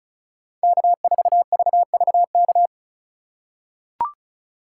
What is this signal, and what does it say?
Morse code, heard as a keyed tone, about 36 wpm, 700 Hz. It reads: K4VVK
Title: USA Call Signs - 5 Characters - 2x Character Spacing - 35wpm